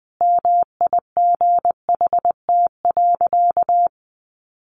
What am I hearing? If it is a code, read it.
Morse code, heard as a keyed tone, about 20 wpm, 700 Hz. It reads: MIGHT.
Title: Sentences from Top 500 Words 20wpm